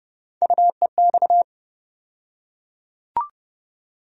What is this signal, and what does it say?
Morse code, heard as a keyed tone, about 30 wpm, 700 Hz. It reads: UEX
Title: USA Call Sign Postfixes - 3 Letters 30wpm